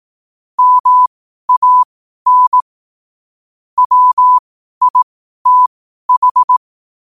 Morse code, heard as a keyed tone, about 18 wpm, spaced as for 12 wpm, 1000 Hz. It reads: MAN WITH
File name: qotdfiles / 12wpm